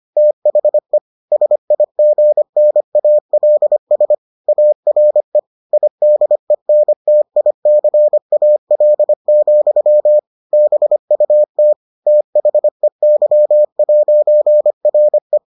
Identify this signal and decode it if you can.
Morse code, heard as a keyed tone, about 25 wpm, 600 Hz. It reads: THE SIGNALS ARE IDENTICAL, BUT THEY'RE